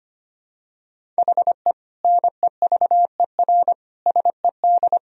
Morse code, heard as a keyed tone, about 25 wpm, 700 Hz. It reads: HE NEVER SED
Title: Sentences from Top 500 Words and Abbreviations 25wpm